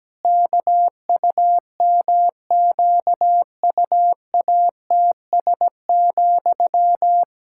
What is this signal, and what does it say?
Morse code, heard as a keyed tone, about 17 wpm, 700 Hz. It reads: KUMQUATS,
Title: Pangram Challenge v2 - 17wpm